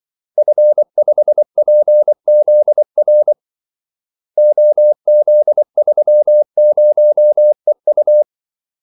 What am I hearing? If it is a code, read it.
Morse code, heard as a keyed tone, about 24 wpm, 600 Hz. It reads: F5PZR OZ30EU
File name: Callsigns-24wpm